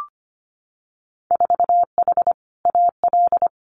Morse code, heard as a keyed tone, about 25 wpm, 700 Hz. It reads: 4H AL
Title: Winter Field Day Exchanges Encore 25wpm